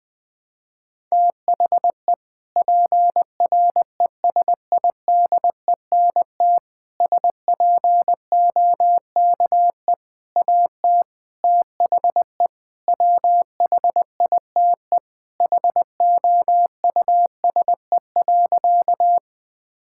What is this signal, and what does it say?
Morse code, heard as a keyed tone, about 21 wpm, 700 Hz. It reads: THE PRESIDENT SPOKE AT THE WHITE HOUSE.
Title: Sentences with 4-Word Phrases - 20wpm